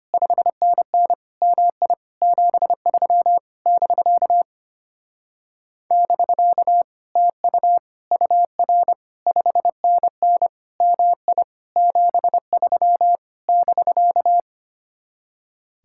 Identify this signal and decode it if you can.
Morse code, heard as a keyed tone, about 27 wpm, 700 Hz. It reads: 5NN MI 73 <BK> <BK> TU UR 5NN MI 73 <BK>
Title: Speed-Racing Common QSO Phrases Encore - 25wpm